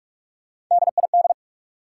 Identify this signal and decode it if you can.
Morse code, heard as a keyed tone, about 45 wpm, 700 Hz. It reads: DID
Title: Sentences from Top 100 Words - New Word Drill - 45wpm